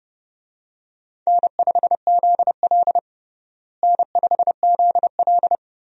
Morse code, heard as a keyed tone, about 30 wpm, 700 Hz. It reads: N5ZL N5ZL
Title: USA Call Signs - 4 Characters - Repeated 2x - 30wpm